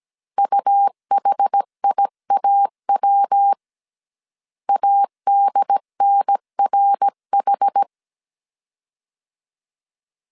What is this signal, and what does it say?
Morse code, heard as a keyed tone, about 17 wpm, 800 Hz. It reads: UHIAW ADNRH